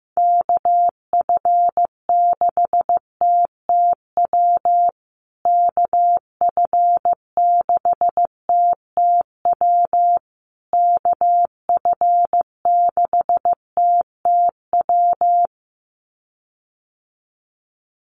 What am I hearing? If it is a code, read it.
Morse code, heard as a keyed tone, about 15 wpm, 700 Hz. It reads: KF6TTW KF6TTW KF6TTW
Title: USA Call Signs Repeated 3x 15wpm